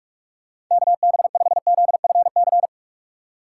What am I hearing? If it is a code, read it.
Morse code, heard as a keyed tone, about 41 wpm, 700 Hz. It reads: KB5ZFC